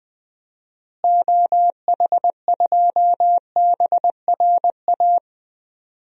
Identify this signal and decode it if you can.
Morse code, heard as a keyed tone, about 20 wpm, 700 Hz. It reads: OH2BRA
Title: DXCC-100 Call Signs - 6 Characters - Repeated 2x - 20wpm